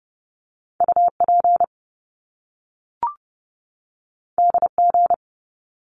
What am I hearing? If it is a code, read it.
Morse code, heard as a keyed tone, about 30 wpm, 700 Hz. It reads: UP DG
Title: Top Two Letter Combinations in Words 30wpm